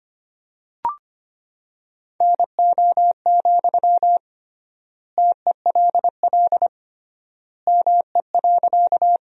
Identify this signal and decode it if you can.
Morse code, heard as a keyed tone, about 25 wpm, 700 Hz. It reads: NO, TELL ME.
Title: 3x Word-Spacing - Sentences from Top 200 Words - 25wpm